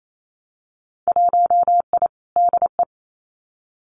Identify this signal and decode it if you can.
Morse code, heard as a keyed tone, about 28 wpm, 700 Hz. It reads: 1I DE
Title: Winter Field Day Exchanges Encore 28wpm